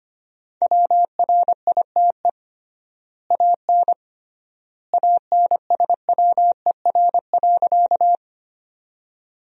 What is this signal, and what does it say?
Morse code, heard as a keyed tone, about 25 wpm, 700 Hz. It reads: WRITE AN ANSWER.
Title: 3x Word-Spacing - Sentences from Top 300 Words - 25wpm